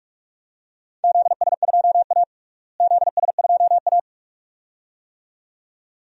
Morse code, heard as a keyed tone, about 48 wpm, 700 Hz. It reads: ZS2A ZS2A
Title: DXCC-100 Call Signs - 4 Characters - Repeated 2x - 45wpm